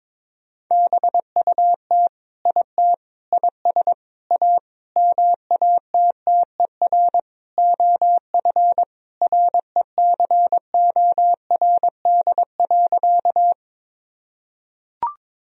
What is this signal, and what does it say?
Morse code, heard as a keyed tone, about 22 wpm, 700 Hz. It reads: BUT IT IS A MATTER OF RECORD.
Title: Sentences from Top 1000 Words - Review Part 3 - 22wpm